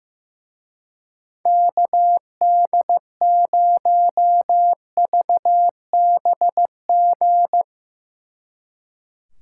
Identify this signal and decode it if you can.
Morse code, heard as a keyed tone, about 16 wpm, 700 Hz. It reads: KD0VBG
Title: Speed Racing - USA 2x3 Call Signs 15wpm